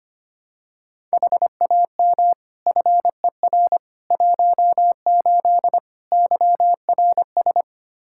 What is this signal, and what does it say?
Morse code, heard as a keyed tone, about 25 wpm, 700 Hz. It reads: HAM FER 18 YRS
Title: Practice QSO - Exchange 3 - More Info - 25wpm